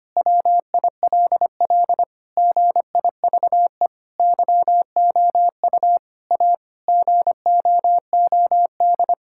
Morse code, heard as a keyed tone, about 25 wpm, 700 Hz. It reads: WILL GIVE YOU A GOOD
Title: Sentences from Top 1000 Words - Review Part 6 - 25wpm